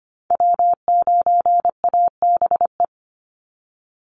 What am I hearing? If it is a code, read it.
Morse code, heard as a keyed tone, about 25 wpm, 700 Hz. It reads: W9ABE
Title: USA Call Signs 25wpm